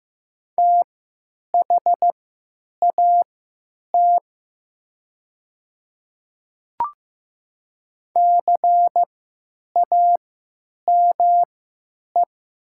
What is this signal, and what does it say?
Morse code, heard as a keyed tone, about 15 wpm, spaced as for 9 wpm, 700 Hz. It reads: THAT CAME
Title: 3x Character Spacing - Top 200 Words - 15wpm